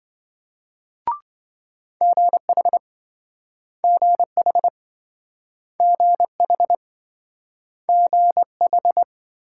Morse code, heard as a keyed tone, about 27 wpm, 700 Hz. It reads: GH GH GH GH